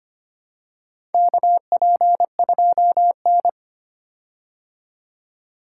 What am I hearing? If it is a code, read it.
Morse code, heard as a keyed tone, about 25 wpm, 700 Hz. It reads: KP2N
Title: K1USN SST QSO Practice - 25wpm